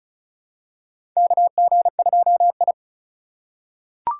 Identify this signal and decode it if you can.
Morse code, heard as a keyed tone, about 36 wpm, 700 Hz. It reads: KG2I
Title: USA Call Signs - 4 Characters - 4x Character Spacing - 35wpm